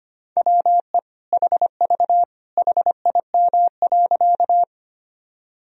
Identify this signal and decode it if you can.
Morse code, heard as a keyed tone, about 25 wpm, 700 Hz. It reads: WE HV HIM.
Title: Sentences from Top 100 Words and Abbreviations 25wpm